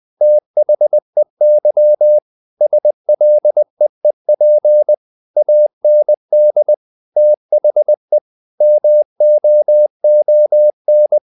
Morse code, heard as a keyed tone, about 20 wpm, 600 Hz. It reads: THEY SLEEP AND THE MOON
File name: the old man and the sea part 2 20wpm